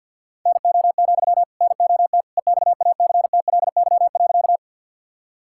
Morse code, heard as a keyed tone, about 50 wpm, 700 Hz. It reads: NO, NOT EXACTLY.
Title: Sentences from Top 1000 Words - Review Part 6 - 50wpm